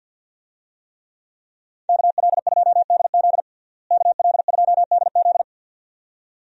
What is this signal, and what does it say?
Morse code, heard as a keyed tone, about 53 wpm, 700 Hz. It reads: KL2DZ KL2DZ
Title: USA Call Signs - 5 Characters - Repeated 2x - 50wpm